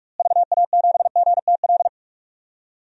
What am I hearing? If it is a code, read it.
Morse code, heard as a keyed tone, about 46 wpm, 700 Hz. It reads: VA7GTL
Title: DXCC-100 Call Signs - 6 Characters - Repeated 3x - 45wpm